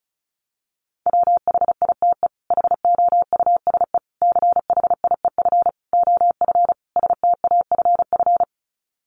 Morse code, heard as a keyed tone, about 34 wpm, 700 Hz. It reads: WHITE HOUSE CHIEF OF STAFF
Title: Sets of 5 Words Encore 35wpm